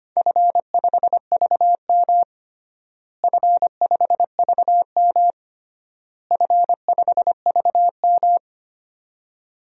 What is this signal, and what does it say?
Morse code, heard as a keyed tone, about 25 wpm, 700 Hz. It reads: F5VM F5VM F5VM
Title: DXCC-100 Call Signs - 4 Characters - Repeated 3x - 25wpm